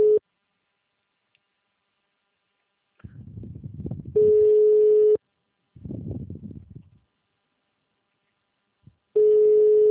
Ringback tone (Europe, 425 Hz).